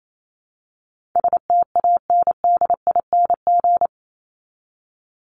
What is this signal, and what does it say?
Morse code, heard as a keyed tone, about 28 wpm, 700 Hz. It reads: STANDING
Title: ING Suffix - Top 100 Words with Base Words 28wpm